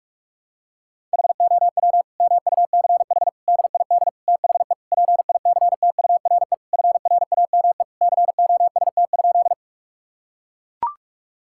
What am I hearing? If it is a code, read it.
Morse code, heard as a keyed tone, about 42 wpm, 700 Hz. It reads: HOW MUCH DID THE PICTURE FRAME COST?
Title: Sentences from Top 1000 Words - Review Part 6 - 45wpm